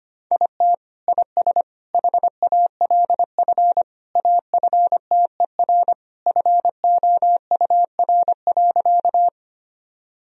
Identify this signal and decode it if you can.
Morse code, heard as a keyed tone, about 25 wpm, 700 Hz. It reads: IT IS HALF AFTER FOUR.
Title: Sentences from Top 1000 Words - Review Part 1 - 25wpm